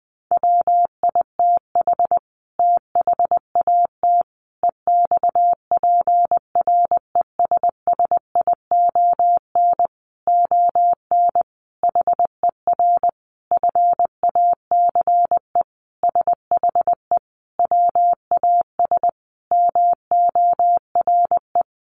Morse code, heard as a keyed tone, about 20 wpm, 700 Hz. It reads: WITH THAT EXPRESSION ON HER FACE SHE WAS MORE
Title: Sentences from Top 1200 Words 20wpm